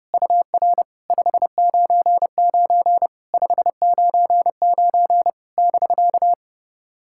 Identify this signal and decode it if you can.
Morse code, heard as a keyed tone, about 30 wpm, 700 Hz. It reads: UR 599 599 <BK>
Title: SOTA QSO Practice 30wpm